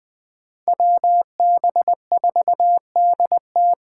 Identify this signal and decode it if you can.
Morse code, heard as a keyed tone, about 20 wpm, 700 Hz. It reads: WB4DT